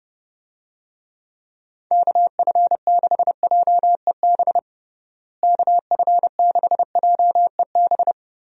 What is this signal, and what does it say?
Morse code, heard as a keyed tone, about 30 wpm, 700 Hz. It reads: KF6JEB KF6JEB
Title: USA Call Signs - 6 Characters - Repeated 2x - 30wpm